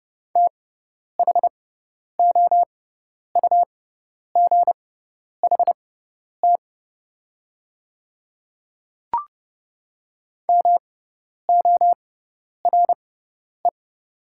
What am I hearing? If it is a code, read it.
Morse code, heard as a keyed tone, about 30 wpm, spaced as for 10 wpm, 700 Hz. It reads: THOUGHT MORE
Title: 6x Character Spacing - Top 500 Words - 30wpm